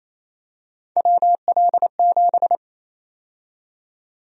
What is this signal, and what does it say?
Morse code, heard as a keyed tone, about 28 wpm, 700 Hz. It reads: WL7
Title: USA Call Sign Prefixes - 28wpm